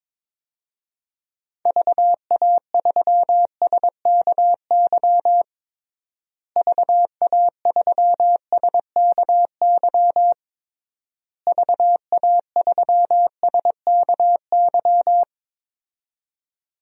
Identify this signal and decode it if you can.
Morse code, heard as a keyed tone, about 22 wpm, 700 Hz. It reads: VA3SKY VA3SKY VA3SKY
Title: DXCC-100 Call Signs - 6 Characters - Repeated 3x - 22wpm